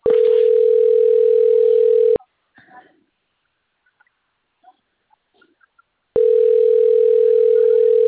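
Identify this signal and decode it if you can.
Ringback tone (North America), 440 + 480 Hz.
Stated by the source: Outgoing call recording from the TLMK - Rio de janeiro 2 BKP phone system.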